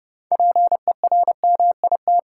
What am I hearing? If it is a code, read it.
Morse code, heard as a keyed tone, about 30 wpm, 700 Hz. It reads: PERMIT
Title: Six Letter Words - 30wpm